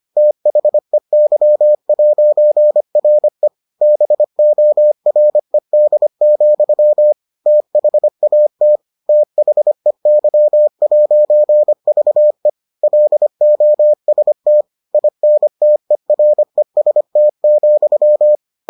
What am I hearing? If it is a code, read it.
Morse code, heard as a keyed tone, about 25 wpm, 600 Hz. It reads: THEY'RE BORED, THAT THEY'VE LOST INTEREST,